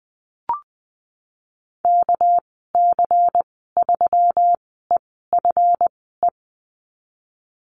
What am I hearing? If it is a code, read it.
Morse code, heard as a keyed tone, about 20 wpm, spaced as for 15 wpm, 700 Hz. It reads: KC3EFE
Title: 213 Lesson 34g - Number 8 - Call Signs 20wpm